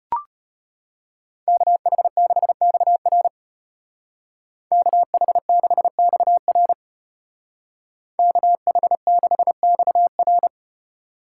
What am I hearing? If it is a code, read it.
Morse code, heard as a keyed tone, about 33 wpm, 700 Hz. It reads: KH6XR KH6XR KH6XR